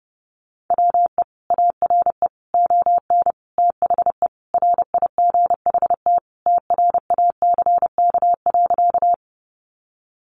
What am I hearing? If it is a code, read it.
Morse code, heard as a keyed tone, about 30 wpm, 700 Hz. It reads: WE ARE ON THE RIGHT TRACK.
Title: Sentences from Top 1100 Words 30wpm